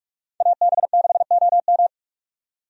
Morse code, heard as a keyed tone, about 45 wpm, 700 Hz. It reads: AB6OM
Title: USA Call Signs - 5 Characters - 45wpm